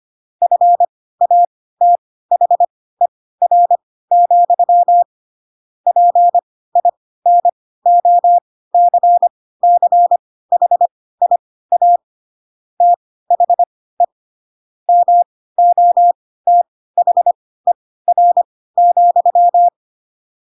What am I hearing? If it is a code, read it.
Morse code, heard as a keyed tone, about 25 wpm, spaced as for 16 wpm, 700 Hz. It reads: FATHER, PINOCCHIA THE MOTHER,
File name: Pinocchio / Pinocchio ch3 10 25 ik1hge